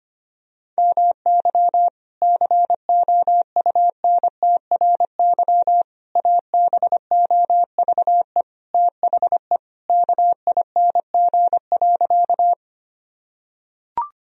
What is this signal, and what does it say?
Morse code, heard as a keyed tone, about 25 wpm, 700 Hz. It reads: MY COUNTRY ABOVE THE KING.
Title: Sentences from Top 1000 Words - Review Part 2 - 25wpm